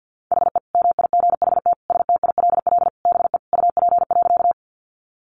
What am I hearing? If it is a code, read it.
Morse code, heard as a keyed tone, about 50 wpm, 700 Hz. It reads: HE MIGHT STILL BE UP.
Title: Sentences from Top 300 Words 50wpm